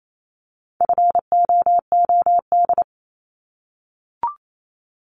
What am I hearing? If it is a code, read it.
Morse code, heard as a keyed tone, about 28 wpm, 700 Hz. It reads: FOOD
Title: Multi-letter Phonemes 28wpm